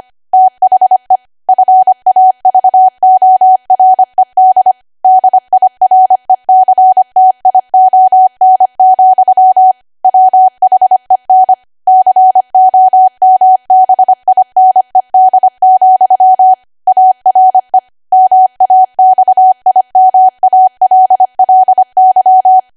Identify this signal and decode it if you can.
Morse code, heard as a keyed tone, about 25 wpm, 750 Hz. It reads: THE FAVORED DIRECTION, WHEN COMBINED, ARE MAXIMALLY
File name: Archive / 25 WPM / 25WPM